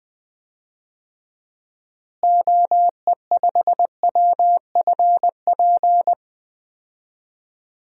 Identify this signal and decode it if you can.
Morse code, heard as a keyed tone, about 20 wpm, 700 Hz. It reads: OE5WFP
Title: DXCC-100 Call Signs - 6 Characters - Repeated 2x - 20wpm